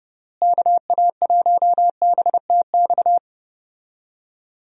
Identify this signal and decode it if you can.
Morse code, heard as a keyed tone, about 30 wpm, 700 Hz. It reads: KA1BTX
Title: USA Call Signs - 6 Characters - Repeated 2x - 30wpm